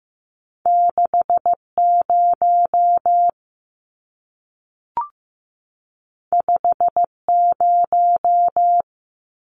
Morse code, heard as a keyed tone, about 15 wpm, 700 Hz. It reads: 6 0 5 0
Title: Numbers - 2-Digit 15wpm